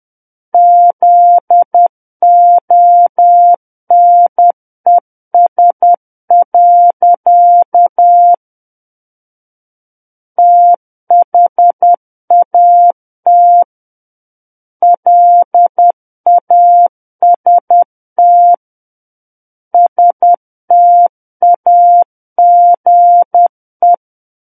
Morse code, heard as a keyed tone, about 10 wpm, 700 Hz. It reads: ZONES. THAT LAST STAGE
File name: Book One / 01-Chapter 1 - 10 WPM